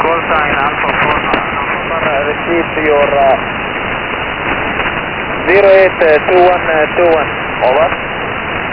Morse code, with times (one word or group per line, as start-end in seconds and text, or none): none